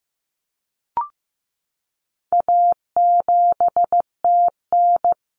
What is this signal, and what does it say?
Morse code, heard as a keyed tone, about 15 wpm, 700 Hz. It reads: A7TN
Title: Common Cut Numbers - 4-Digit 15wpm